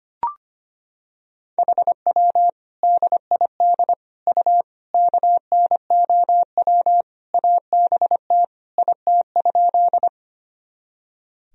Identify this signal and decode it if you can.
Morse code, heard as a keyed tone, about 25 wpm, 700 Hz. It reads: HW DID U KNOW ABT IT?
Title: Sentences from Top 100 Words and Abbreviations 25wpm